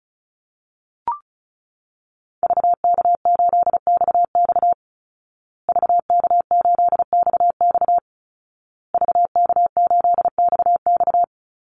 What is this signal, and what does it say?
Morse code, heard as a keyed tone, about 34 wpm, 700 Hz. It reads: VK8XX VK8XX VK8XX
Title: Call Signs Repeated 2x - 35wpm